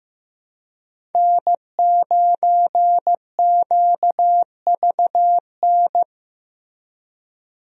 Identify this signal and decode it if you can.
Morse code, heard as a keyed tone, about 15 wpm, 700 Hz. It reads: N9QVN
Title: USA Call Signs - 5 Characters - Repeated 2x - 15wpm